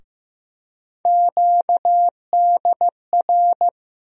Morse code, heard as a keyed tone, about 15 wpm, 700 Hz. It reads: QDR
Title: USA Call Sign Postfixes - 3 Letters 15wpm